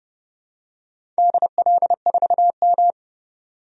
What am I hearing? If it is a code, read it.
Morse code, heard as a keyed tone, about 30 wpm, 700 Hz. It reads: DL4M
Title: DXCC-100 Call Signs - 4 Characters - 30wpm